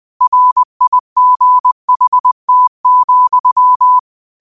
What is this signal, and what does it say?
Morse code, heard as a keyed tone, about 20 wpm, 1000 Hz. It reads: RIGHT,
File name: qotdfiles / 20wpm